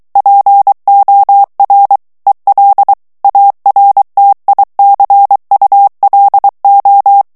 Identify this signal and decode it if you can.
Morse code, heard as a keyed tone, about 23 wpm, 800 Hz. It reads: POR EL ARTICULO
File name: morse mp3 / ESPAGNOL TEXTE EN CLAIR 2 VITESSE 22 MOTS MINUTES